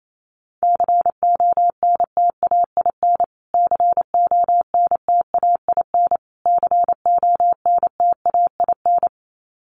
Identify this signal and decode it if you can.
Morse code, heard as a keyed tone, about 28 wpm, 700 Hz. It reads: CONTAIN CONTAIN CONTAIN
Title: Top 500 Words Repeated 28wpm